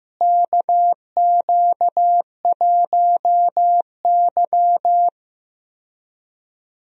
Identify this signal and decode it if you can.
Morse code, heard as a keyed tone, about 15 wpm, 700 Hz. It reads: KQ1Y
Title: USA Call Signs - 4 Characters - Repeated 2x - 15wpm